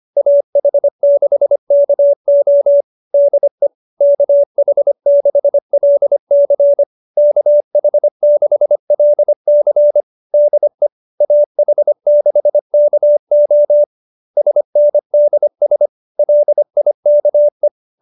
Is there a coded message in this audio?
Yes: Morse code from 0.2 to 17.7 s, heard as a keyed tone, about 25 wpm, 570 Hz. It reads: AH6KO DE KH6LC KH6LC DE AH6KO SNDS LIKE